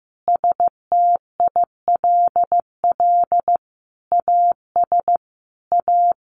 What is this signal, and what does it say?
Morse code, heard as a keyed tone, about 15 wpm, 700 Hz. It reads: STILL AS A
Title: Sentences from Top 700 Words 15wpm